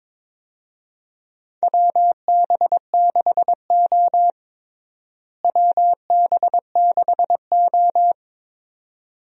Morse code, heard as a keyed tone, about 22 wpm, 700 Hz. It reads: WB6O WB6O
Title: USA Call Signs - 4 Characters - Repeated 2x - 22wpm